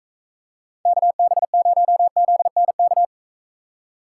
Morse code, heard as a keyed tone, about 45 wpm, 700 Hz. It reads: KB0ZNK